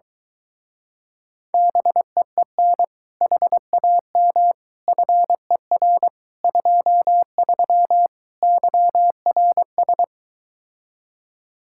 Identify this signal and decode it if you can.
Morse code, heard as a keyed tone, about 22 wpm, 700 Hz. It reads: BEEN HAM FER 23 YRS